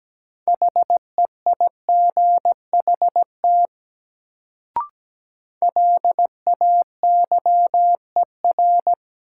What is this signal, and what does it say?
Morse code, heard as a keyed tone, about 17 wpm, 700 Hz. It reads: HEIGHT LAYER
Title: Sentences from Top 1600 Words - New Word Drill - 17wpm